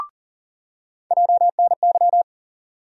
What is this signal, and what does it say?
Morse code, heard as a keyed tone, about 40 wpm, 700 Hz. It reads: JNY
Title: USA Call Sign Postfixes - 3 Letters 40wpm